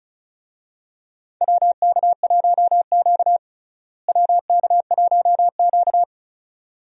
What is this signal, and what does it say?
Morse code, heard as a keyed tone, about 36 wpm, 700 Hz. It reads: WK1Q WK1Q
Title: USA Call Signs - 4 Characters - Repeated 2x - 35wpm